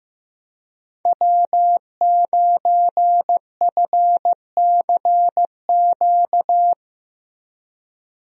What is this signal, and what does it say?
Morse code, heard as a keyed tone, about 15 wpm, 700 Hz. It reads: W9FCQ
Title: USA Call Signs - 5 Characters - Repeated 3x - 15wpm